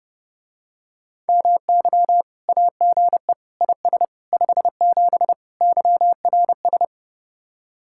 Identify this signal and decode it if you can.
Morse code, heard as a keyed tone, about 30 wpm, 700 Hz. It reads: MY AGE IS 57 YRS